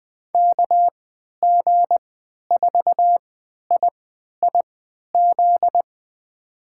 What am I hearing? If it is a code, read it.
Morse code, heard as a keyed tone, about 21 wpm, spaced as for 11 wpm, 700 Hz. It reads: KG4IIZ
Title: USA Call Signs - 6 Characters - 3x Character Spacing - 20wpm